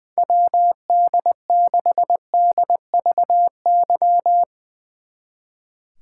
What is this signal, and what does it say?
Morse code, heard as a keyed tone, about 20 wpm, 700 Hz. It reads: WD6DVY
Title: Speed Racing - USA 2x3 Call Signs 20wpm